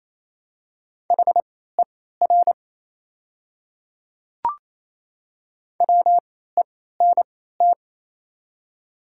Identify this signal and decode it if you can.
Morse code, heard as a keyed tone, about 27 wpm, spaced as for 16 wpm, 700 Hz. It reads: HER WENT
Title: 3x Character Spacing - Top 300 Words - 28wpm